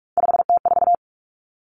Morse code, heard as a keyed tone, about 46 wpm, 700 Hz. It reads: 5T4
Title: Three Challenging Letter-Number Combinations 45wpm